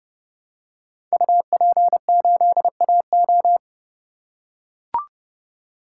Morse code, heard as a keyed tone, about 30 wpm, 700 Hz. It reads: UP8AO
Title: Code Groups - 5 Letter:Numbers - 30wpm